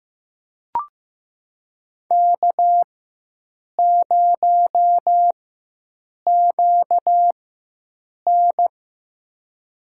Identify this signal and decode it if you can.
Morse code, heard as a keyed tone, about 15 wpm, spaced as for 7 wpm, 700 Hz. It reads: K0QN
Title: USA Call Signs - 4 Characters - 4x Character Spacing - 15wpm